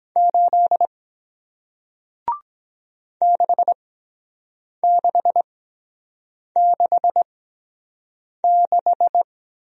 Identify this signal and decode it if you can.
Morse code, heard as a keyed tone, about 23 wpm, spaced as for 7 wpm, 700 Hz. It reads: 8 6666